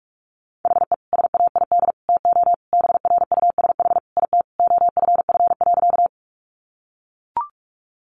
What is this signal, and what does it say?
Morse code, heard as a keyed tone, about 44 wpm, 700 Hz. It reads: HE SAID TO BRUSH IT OFF.
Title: Sentences from Top 1100 Words 45wpm